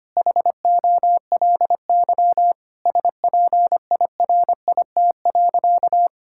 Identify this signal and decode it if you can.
Morse code, heard as a keyed tone, about 25 wpm, 700 Hz. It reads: HOLY SPIRIT.
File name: practice / 25wpm / Sentences with 9-Word Phrases - 25wpm